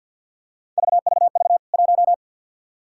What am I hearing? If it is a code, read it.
Morse code, heard as a keyed tone, about 53 wpm, 700 Hz. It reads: VVV 1
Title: K1USN SST QSO Practice 50wpm